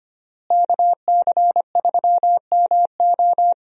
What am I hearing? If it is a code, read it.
Morse code, heard as a keyed tone, about 25 wpm, 700 Hz. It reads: KC3MO
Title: USA Call Signs - 5 Characters - Repeated 3x - 25wpm